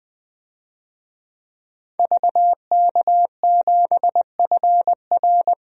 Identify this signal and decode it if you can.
Morse code, heard as a keyed tone, about 20 wpm, 700 Hz. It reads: VK7FR
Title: DXCC-100 Call Signs - 5 Characters - 20wpm